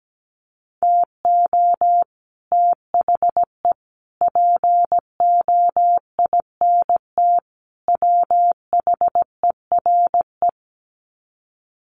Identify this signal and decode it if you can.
Morse code, heard as a keyed tone, about 17 wpm, 700 Hz. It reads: TO THE POINT WHERE
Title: Sets of 4 Words 17wpm